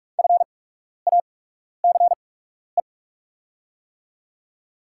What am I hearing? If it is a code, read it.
Morse code, heard as a keyed tone, about 44 wpm, spaced as for 12 wpm, 690 Hz. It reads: FACE
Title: 8x Character Spacing - Top 500 Words - 45wpm